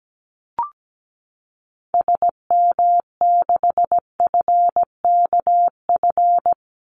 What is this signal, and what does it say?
Morse code, heard as a keyed tone, about 17 wpm, 700 Hz. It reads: SM6FKF
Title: Call Signs - 17wpm